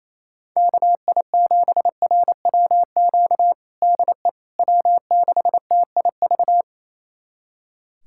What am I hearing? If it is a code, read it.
Morse code, heard as a keyed tone, about 28 wpm, 700 Hz. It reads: KI7RWQ DE W6TIV
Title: USA Callsign DE Callsign 28wpm